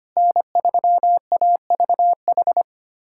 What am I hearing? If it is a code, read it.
Morse code, heard as a keyed tone, about 25 wpm, 700 Hz. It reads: N3AVH
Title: USA Callsign DE Callsign - 5 Characters - 25wpm